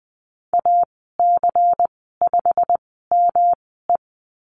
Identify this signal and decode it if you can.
Morse code, heard as a keyed tone, about 20 wpm, spaced as for 15 wpm, 700 Hz. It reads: AC5ME
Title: 122 Lesson 20g - Number 3 - Call Signs 20wpm